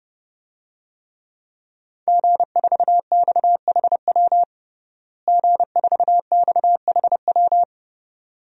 Morse code, heard as a keyed tone, about 32 wpm, 700 Hz. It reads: G4XHW G4XHW
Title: DXCC-100 Call Signs - 5 Characters - Repeated 2x - 30wpm